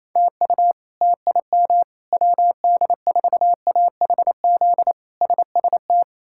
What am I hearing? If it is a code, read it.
Morse code, heard as a keyed tone, about 28 wpm, 700 Hz. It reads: TU TIM WD4AHZ SST
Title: K1USN SST QSO Practice - 28wpm